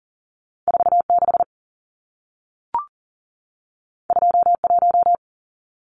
Morse code, heard as a keyed tone, about 40 wpm, 700 Hz. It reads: 46 21
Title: Numbers - 2-Digit 40wpm